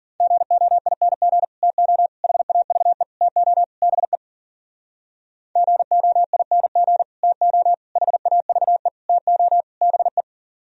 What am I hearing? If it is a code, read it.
Morse code, heard as a keyed tone, about 54 wpm, 700 Hz. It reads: GOING TO HAVE TO BE GOING TO HAVE TO BE